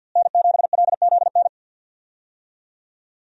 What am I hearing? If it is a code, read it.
Morse code, heard as a keyed tone, about 52 wpm, 700 Hz. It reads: N7LZN
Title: USA Call Signs - 5 Characters - Repeated 2x - 50wpm